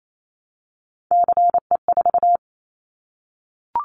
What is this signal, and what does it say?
Morse code, heard as a keyed tone, about 28 wpm, 700 Hz. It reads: CE4
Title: Callsign Prefixes for Top 100 DXCC - 28wpm